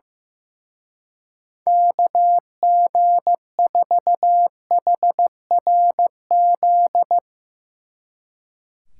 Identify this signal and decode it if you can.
Morse code, heard as a keyed tone, about 16 wpm, 700 Hz. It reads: KG4HRZ
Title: Speed Racing - USA Call Signs 15wpm